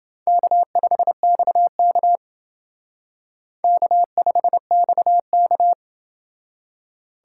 Morse code, heard as a keyed tone, about 29 wpm, 700 Hz. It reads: K5XK K5XK